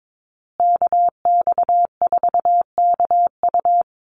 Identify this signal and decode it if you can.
Morse code, heard as a keyed tone, about 22 wpm, 690 Hz. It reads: KX4KU
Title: Call Signs Repeated 2x - 22wpm